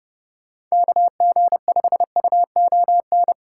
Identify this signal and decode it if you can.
Morse code, heard as a keyed tone, about 40 wpm, 700 Hz. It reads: KG5UON